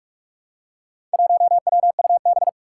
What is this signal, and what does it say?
Morse code, heard as a keyed tone, about 46 wpm, 700 Hz. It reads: 1WUB
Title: Four Challenging Letter-Number Combinations 45wpm